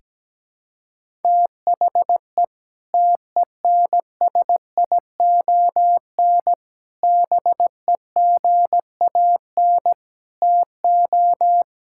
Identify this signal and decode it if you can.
Morse code, heard as a keyed tone, about 17 wpm, 700 Hz. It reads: THE TENSION BEGAN TO
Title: Sentences from Top 1600 Words 17wpm